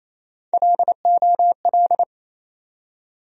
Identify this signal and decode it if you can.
Morse code, heard as a keyed tone, about 28 wpm, 700 Hz. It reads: LOL
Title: USA Call Sign Postfixes - 3 Letters 28wpm